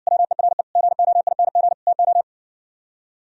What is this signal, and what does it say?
Morse code, heard as a keyed tone, about 60 wpm, 700 Hz. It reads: WERE GOING TO